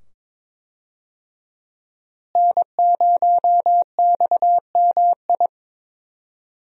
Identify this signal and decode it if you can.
Morse code, heard as a keyed tone, about 22 wpm, 700 Hz. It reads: N0XMI
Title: USA Call Signs - 5 Characters - Repeated 2x - 22wpm